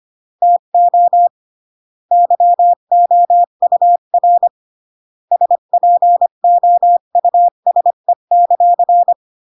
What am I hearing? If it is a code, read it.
Morse code, heard as a keyed tone, about 25 wpm, 700 Hz. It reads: TO YOUR SPOUSE;